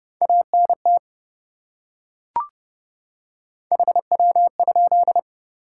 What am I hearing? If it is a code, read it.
Morse code, heard as a keyed tone, about 28 wpm, 700 Hz. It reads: ANT HW?